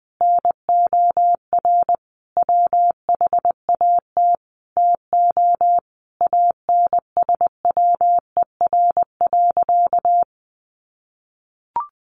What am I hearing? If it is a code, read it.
Morse code, heard as a keyed tone, about 20 wpm, 700 Hz. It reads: NOR WHAT TO ANSWER.
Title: Sentences from Top 1500 Words 20wpm